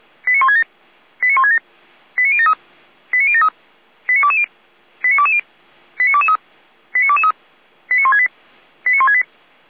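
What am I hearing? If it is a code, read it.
Five-tone selective call (ZVEI), heard as quick runs of five tones, tones of about 80 ms. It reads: 89167 89167 89073 89073 89220 89220 89223 89223 89168 89168
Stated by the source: ZVEI five-tone selective call received by SDR.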